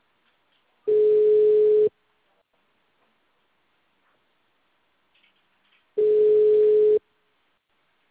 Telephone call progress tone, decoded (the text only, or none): ringback tone